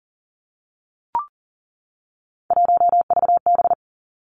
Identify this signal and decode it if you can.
Morse code, heard as a keyed tone, about 40 wpm, 700 Hz. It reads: 1VB
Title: Three Challenging Letter-Number Combinations 40wpm